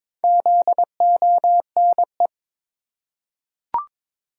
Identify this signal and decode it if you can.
Morse code, heard as a keyed tone, about 22 wpm, 700 Hz. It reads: ZONE
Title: Four Letter Words - 22wpm